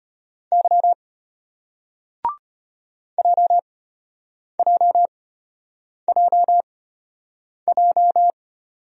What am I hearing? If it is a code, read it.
Morse code, heard as a keyed tone, about 34 wpm, spaced as for 8 wpm, 700 Hz. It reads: Y JJJJ